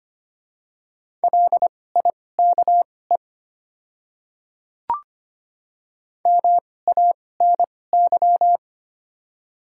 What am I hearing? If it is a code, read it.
Morse code, heard as a keyed tone, about 25 wpm, spaced as for 18 wpm, 690 Hz. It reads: LIKE MANY
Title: 2x Character Spacing - Top 100 Words - 25wpm